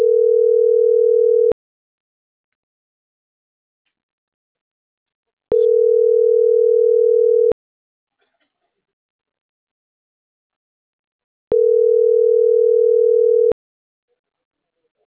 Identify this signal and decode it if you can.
Ringback tone (North America), 440 + 480 Hz.